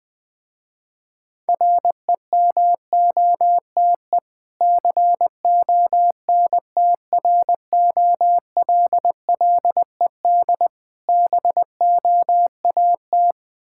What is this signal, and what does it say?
Morse code, heard as a keyed tone, about 20 wpm, 700 Hz. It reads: REMOTE CONTROLLED BOAT
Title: Difficult Words with Hints - 20wpm